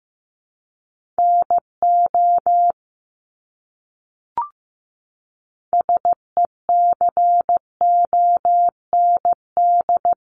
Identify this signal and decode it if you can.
Morse code, heard as a keyed tone, about 14 wpm, 700 Hz. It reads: NO SECOND
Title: Top 500 Words Repeated 15wpm